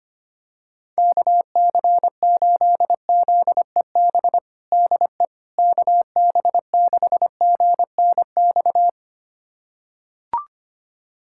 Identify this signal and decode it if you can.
Morse code, heard as a keyed tone, about 25 wpm, 700 Hz. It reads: KC8ZEB DE KB6GNX
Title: USA Callsign DE Callsign 25wpm